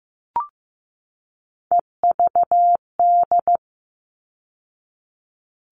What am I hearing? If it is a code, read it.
Morse code, heard as a keyed tone, about 15 wpm, 700 Hz. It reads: EVD
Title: All Cut Numbers - 3-Digit 15wpm